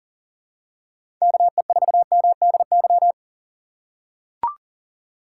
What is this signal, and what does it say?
Morse code, heard as a keyed tone, about 40 wpm, 700 Hz. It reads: KE4MDY
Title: USA Call Signs - 4x Character Spacing - 40wpm